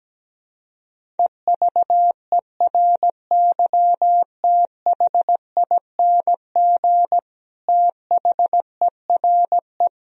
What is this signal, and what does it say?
Morse code, heard as a keyed tone, about 17 wpm, 700 Hz. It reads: EVERYTHING THERE
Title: Sentences from Top 1000 Words - Review Part 2 - 17wpm